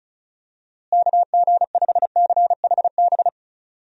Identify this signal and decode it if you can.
Morse code, heard as a keyed tone, about 36 wpm, 700 Hz. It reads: KG5CHB
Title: USA Call Signs - 6 Characters - Repeated 2x - 35wpm